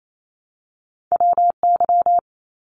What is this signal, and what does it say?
Morse code, heard as a keyed tone, about 28 wpm, 700 Hz. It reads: WY
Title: ARRL Sections 28wpm